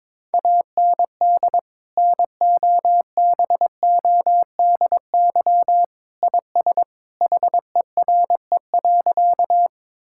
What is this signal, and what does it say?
Morse code, heard as a keyed tone, about 22 wpm, 700 Hz. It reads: AND NOBODY IS HERE.
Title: Sentences from Top 1600 Words 22wpm